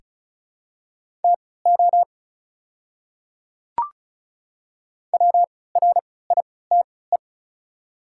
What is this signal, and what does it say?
Morse code, heard as a keyed tone, about 36 wpm, spaced as for 20 wpm, 700 Hz. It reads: TO WRITE
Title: 3x Character Spacing - Top 100 Words - 35wpm